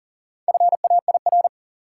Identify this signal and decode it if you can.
Morse code, heard as a keyed tone, about 40 wpm, 700 Hz. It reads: FAIR
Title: Four Letter Words - 40wpm